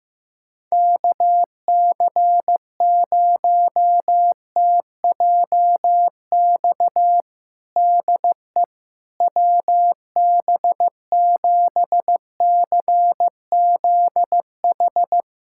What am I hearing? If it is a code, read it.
Morse code, heard as a keyed tone, about 15 wpm, 700 Hz. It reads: KC0TJX DE WB7CZH
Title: USA Callsign DE Callsign 15wpm